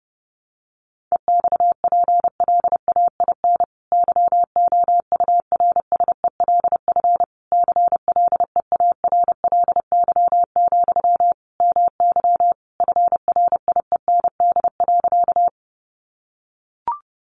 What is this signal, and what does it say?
Morse code, heard as a keyed tone, about 30 wpm, 700 Hz. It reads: EXPLAIN YOURSELF CLEARLY, MY FRIEND.
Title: Sentences from Top 1300 Words 30wpm